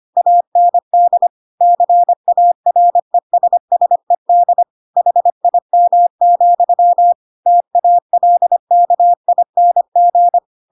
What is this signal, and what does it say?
Morse code, heard as a keyed tone, about 25 wpm, 700 Hz. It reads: AND CARESSED HIM, TALKING
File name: Pinocchio / Pinocchio ch7 25 25 ik1hge